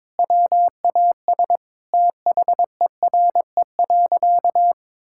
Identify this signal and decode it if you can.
Morse code, heard as a keyed tone, about 22 wpm, 700 Hz. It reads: WAS THERE.
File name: practice / 22wpm / Sentences with 4-Word Phrases - 22wpm